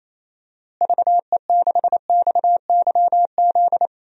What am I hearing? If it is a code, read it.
Morse code, heard as a keyed tone, about 27 wpm, 700 Hz. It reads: VE6XYZ
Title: DXCC-100 Call Signs - 6 Characters - Repeated 3x - 28wpm